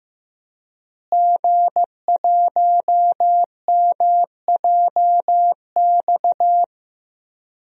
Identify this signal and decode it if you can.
Morse code, heard as a keyed tone, about 16 wpm, 700 Hz. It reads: G1MJX
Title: DXCC-100 Call Signs - 5 Characters - Repeated 3x - 15wpm